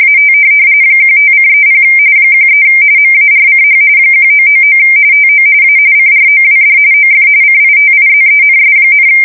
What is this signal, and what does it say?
RTTY, about 45.45 baud. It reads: AT 15:30 13 MARCH 1999 TO A CROWD OF CITIZENS,
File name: fsk170 / USS-Massachusetts